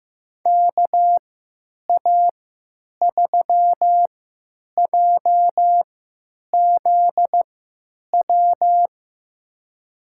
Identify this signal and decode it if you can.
Morse code, heard as a keyed tone, about 15 wpm, spaced as for 8 wpm, 700 Hz. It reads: KA3JZW
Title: USA Call Signs - 6 Characters - 3x Character Spacing - 15wpm